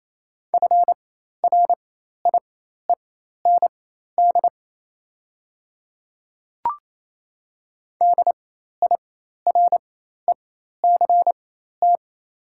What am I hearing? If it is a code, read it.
Morse code, heard as a keyed tone, about 28 wpm, spaced as for 13 wpm, 700 Hz. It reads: FRIEND DIRECT
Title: 4x Character Spacing - Top 500 Words - 28wpm